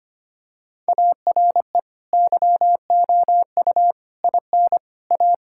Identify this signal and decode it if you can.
Morse code, heard as a keyed tone, about 25 wpm, 700 Hz. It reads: ARE YOU IN A
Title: Sentences from Top 1000 Words - Review Part 6 - 25wpm